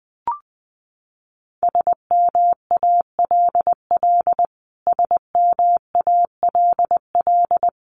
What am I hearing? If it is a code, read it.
Morse code, heard as a keyed tone, about 20 wpm, 700 Hz. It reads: SMALL SMALL
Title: Top 500 Words Repeated 20wpm